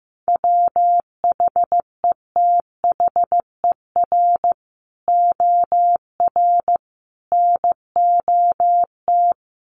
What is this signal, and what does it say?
Morse code, heard as a keyed tone, about 15 wpm, 700 Hz. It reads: WHETHER OR NOT
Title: Sets of 3 Words 15wpm